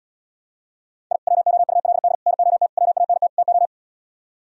Morse code, heard as a keyed tone, about 77 wpm, 700 Hz. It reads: I WOULD NOT WANT TO